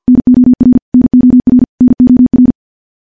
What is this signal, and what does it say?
Morse code, heard as a keyed tone, about 50 wpm, 250 Hz. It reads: NOR NOR NOR
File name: morse / morse-training-50WPM